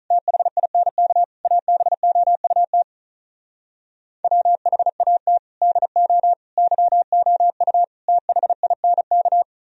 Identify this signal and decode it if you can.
Morse code, heard as a keyed tone, about 46 wpm, 700 Hz. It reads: THINK ABOUT WHAT DO YOU THINK